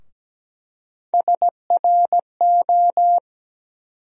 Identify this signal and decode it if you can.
Morse code, heard as a keyed tone, about 17 wpm, 700 Hz. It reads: SRO
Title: USA Call Sign Postfixes - 3 Letters 17wpm